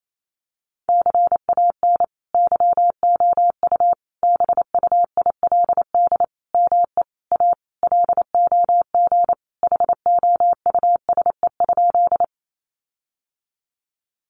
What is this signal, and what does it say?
Morse code, heard as a keyed tone, about 28 wpm, 700 Hz. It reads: CAN YOU BUILD ME A LOG HOUSE?
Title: Sentences from Top 1400 Words 28wpm